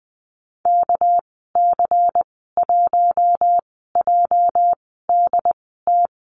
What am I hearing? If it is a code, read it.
Morse code, heard as a keyed tone, about 20 wpm, spaced as for 15 wpm, 700 Hz. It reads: KC1JDT
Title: 207 Lesson 33g - Letter j - Call Signs 20wpm